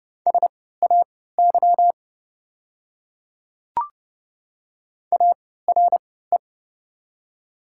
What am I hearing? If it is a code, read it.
Morse code, heard as a keyed tone, about 30 wpm, spaced as for 17 wpm, 700 Hz. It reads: SAY ARE
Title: 3x Character Spacing - Top 500 Words - 30wpm